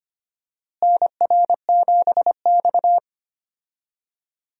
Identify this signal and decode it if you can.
Morse code, heard as a keyed tone, about 25 wpm, 700 Hz. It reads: NR7X
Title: USA Call Signs - 4 Characters - Repeated 3x - 25wpm